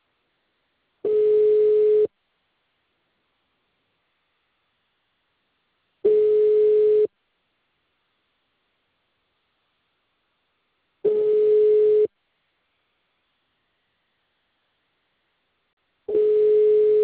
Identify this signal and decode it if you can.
Ringback tone (Europe, 425 Hz).